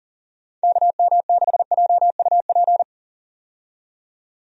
Morse code, heard as a keyed tone, about 40 wpm, 700 Hz. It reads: KM6JUP
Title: USA Call Signs - 6 Characters - 40wpm